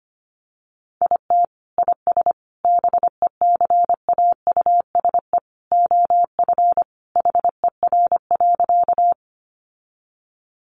Morse code, heard as a keyed tone, about 25 wpm, 700 Hz. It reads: IT IS BECAUSE OF HER.
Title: Sentences from Top 600 Words 25wpm